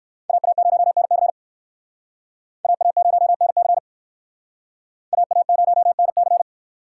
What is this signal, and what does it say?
Morse code, heard as a keyed tone, about 62 wpm, 700 Hz. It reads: AA0NC AA0NC AA0NC